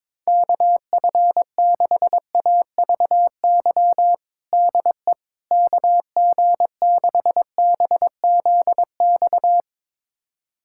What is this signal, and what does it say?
Morse code, heard as a keyed tone, about 22 wpm, 700 Hz. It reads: KF6AVY DE KG6BZX
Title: USA Callsign DE Callsign - 6 Characters - 22wpm